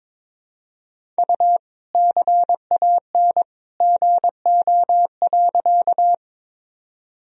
Morse code, heard as a keyed tone, about 22 wpm, 700 Hz. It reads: U CAN GO.
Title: Sentences from Top 100 Words and Abbreviations 22wpm